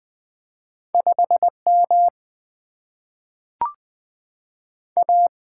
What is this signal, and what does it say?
Morse code, heard as a keyed tone, about 20 wpm, 700 Hz. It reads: 5 M A
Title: Single-Letter Single-Number 20wpm